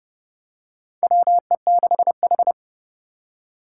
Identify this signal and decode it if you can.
Morse code, heard as a keyed tone, about 32 wpm, 700 Hz. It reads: WE6H
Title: USA Call Signs - 4 Characters - 2x Character Spacing - 30wpm